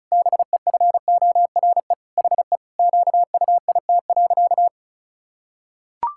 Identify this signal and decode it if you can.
Morse code, heard as a keyed tone, about 34 wpm, 700 Hz. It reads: BEFORE HE QUIT.
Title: Pangram Challenge v2 - 35wpm